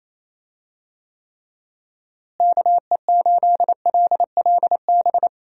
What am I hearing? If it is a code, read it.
Morse code, heard as a keyed tone, about 28 wpm, 700 Hz. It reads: KE8LLB
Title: USA Call Signs - 6 Characters - Repeated 2x - 28wpm